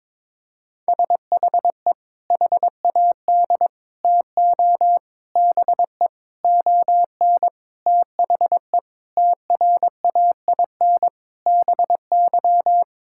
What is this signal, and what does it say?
Morse code, heard as a keyed tone, about 21 wpm, 700 Hz. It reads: SHE HAD TO BE ON THE TRAIN BY
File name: practice / 22wpm / Sentences with 4-Word Phrases - 22wpm